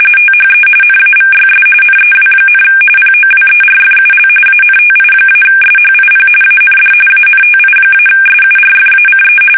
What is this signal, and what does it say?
RTTY, about 45.45 baud. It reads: THE NAVYS NEW, POWERFUL ANTI-SHIP AND ANTI-SURFACE
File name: fsk850 / battleshipnj